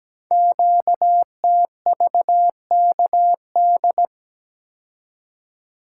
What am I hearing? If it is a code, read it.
Morse code, heard as a keyed tone, about 17 wpm, 700 Hz. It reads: QTVKD
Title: Code Groups - 5 Letter:Numbers - 17wpm